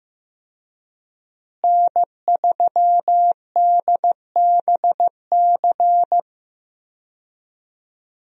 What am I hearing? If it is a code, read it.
Morse code, heard as a keyed tone, about 15 wpm, 700 Hz. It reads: N3DBC
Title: USA Call Signs - 5 Characters - Repeated 2x - 15wpm